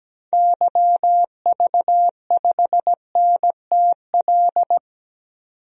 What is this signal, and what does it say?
Morse code, heard as a keyed tone, about 17 wpm, 700 Hz. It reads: YV5NTL
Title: DXCC-100 Call Signs - 6 Characters - 17wpm